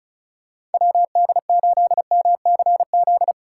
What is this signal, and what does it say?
Morse code, heard as a keyed tone, about 34 wpm, 700 Hz. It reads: WD8MCZ
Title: USA Call Signs - 6 Characters - 4x Character Spacing - 35wpm